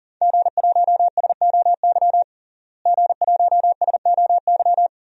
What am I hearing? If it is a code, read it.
Morse code, heard as a keyed tone, about 40 wpm, 700 Hz. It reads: G1SOY G1SOY
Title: DXCC-100 Call Signs - 5 Characters - Repeated 2x - 40wpm